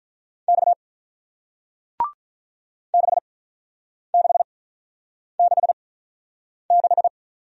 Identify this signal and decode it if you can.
Morse code, heard as a keyed tone, about 48 wpm, spaced as for 9 wpm, 700 Hz. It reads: X 6666